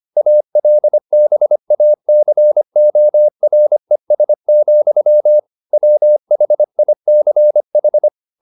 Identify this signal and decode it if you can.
Morse code, heard as a keyed tone, about 25 wpm, 600 Hz. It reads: ALBACORES, WHICH
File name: the old man and the sea part 1 25wpm